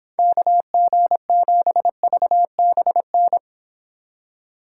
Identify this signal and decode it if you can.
Morse code, heard as a keyed tone, about 25 wpm, 700 Hz. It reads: KG7VBN